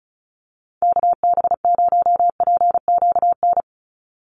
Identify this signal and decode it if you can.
Morse code, heard as a keyed tone, about 34 wpm, 700 Hz. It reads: KB0PQN
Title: USA Call Signs 35wpm